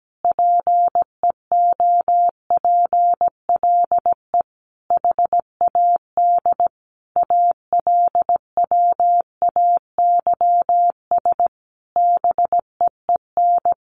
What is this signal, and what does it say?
Morse code, heard as a keyed tone, about 17 wpm, 700 Hz. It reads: PEOPLE HAD ALWAYS BEEN
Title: Sentences from Top 1400 Words 17wpm